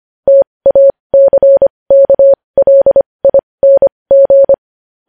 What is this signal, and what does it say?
Morse code, heard as a keyed tone, about 25 wpm, spaced as for 20 wpm, 550 Hz. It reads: TACKLING
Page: news-headlines-in-morse-code-at-20-wpm